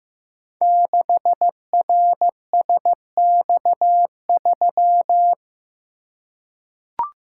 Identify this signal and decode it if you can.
Morse code, heard as a keyed tone, about 15 wpm, 700 Hz. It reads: 6RSX3
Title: Code Groups - 5 Letter:Numbers - 15wpm